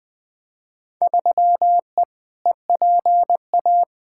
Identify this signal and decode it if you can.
Morse code, heard as a keyed tone, about 20 wpm, 700 Hz. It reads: 3E EPA
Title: Field Day Exchanges Encore - 20wpm